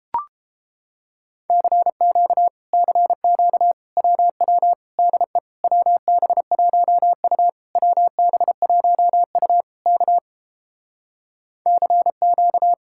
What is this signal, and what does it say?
Morse code, heard as a keyed tone, about 30 wpm, 700 Hz. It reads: CQ CQ WW DE WB1U WB1U K CQ